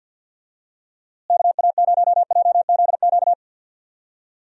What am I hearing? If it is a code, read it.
Morse code, heard as a keyed tone, about 52 wpm, 700 Hz. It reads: KA0JZQ
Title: USA Call Signs - 6 Characters - Repeated 3x - 50wpm